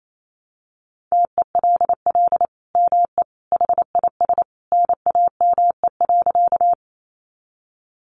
Morse code, heard as a keyed tone, about 28 wpm, 700 Hz. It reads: TELL ME HIS NAME.
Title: Sentences from Top 200 Words 28wpm